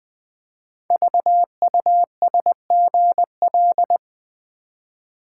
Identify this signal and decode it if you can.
Morse code, heard as a keyed tone, about 20 wpm, 700 Hz. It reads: VUSGL
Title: Code Groups - 5 Letters - 20wpm